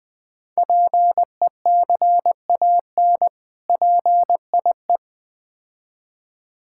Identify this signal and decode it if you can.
Morse code, heard as a keyed tone, about 20 wpm, 700 Hz. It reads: PECAN PIE
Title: Difficult Words with Hints - 20wpm